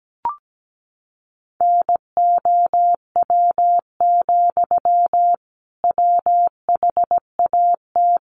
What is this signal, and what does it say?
Morse code, heard as a keyed tone, about 17 wpm, 700 Hz. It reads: NOW, WHAT
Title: Sentences from Top 200 Words 17wpm